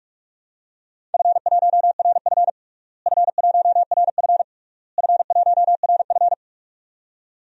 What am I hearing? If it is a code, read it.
Morse code, heard as a keyed tone, about 46 wpm, 700 Hz. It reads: F1RF F1RF F1RF
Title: DXCC-100 Call Signs - 4 Characters - Repeated 3x - 45wpm